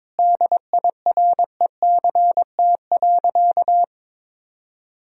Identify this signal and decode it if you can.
Morse code, heard as a keyed tone, about 21 wpm, 700 Hz. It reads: DIRECT.
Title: Sentences from Top 1000 Words - Review Part 2 - 22wpm